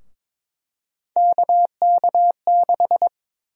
Morse code, heard as a keyed tone, about 22 wpm, 700 Hz. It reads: KK6
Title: USA Call Sign Prefixes - 22wpm